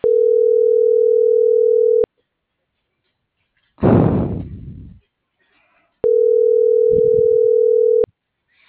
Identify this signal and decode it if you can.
Ringback tone (North America), 440 + 480 Hz.
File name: CENTRAL MULTIMODAL / out alfredovelazquez - .73021